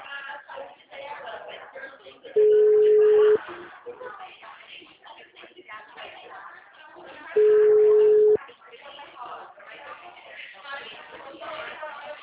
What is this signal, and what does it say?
Ringback tone (Europe, 425 Hz), with speech.